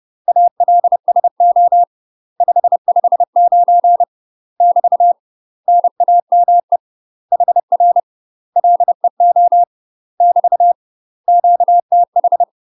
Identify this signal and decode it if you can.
Morse code, heard as a keyed tone, about 30 wpm, 700 Hz. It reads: ALSO 559 <BT> NAME HR LEO <BT> QTH
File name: UK MASTER QSOs TRAINING / 30WPM / UK MASTER QSOs 0011